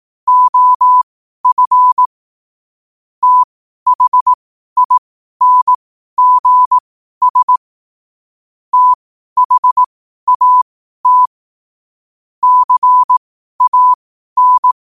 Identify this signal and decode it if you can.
Morse code, heard as a keyed tone, about 18 wpm, spaced as for 12 wpm, 1000 Hz. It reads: OF THINGS THAT CAN
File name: qotdfiles / 12wpm